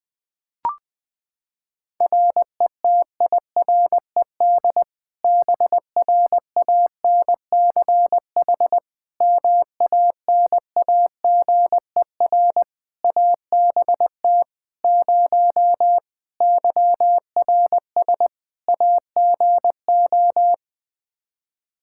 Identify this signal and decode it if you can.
Morse code, heard as a keyed tone, about 21 wpm, 700 Hz. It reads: RETIRED BRANCH MANAGER ABT 0 YRS AGO
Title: Practice QSO - Exchange 3 - More Info - 20wpm